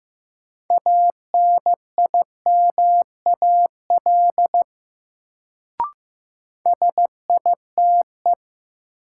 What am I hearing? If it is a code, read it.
Morse code, heard as a keyed tone, about 15 wpm, 700 Hz. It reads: ANIMAL SITE
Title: Sentences from Top 1000 Words - Review Part 1 - New Word Drill - 15wpm